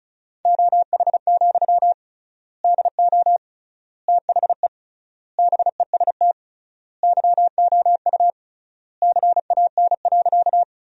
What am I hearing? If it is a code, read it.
Morse code, heard as a keyed tone, about 34 wpm, 700 Hz. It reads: OH, DO THE BEST YOU CAN.
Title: 3x Word-Spacing - Sentences from Top 500 Words - 35wpm